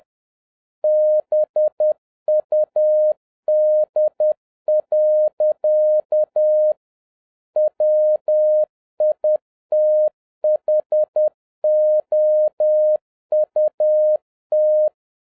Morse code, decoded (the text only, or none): BUD. WITHOUT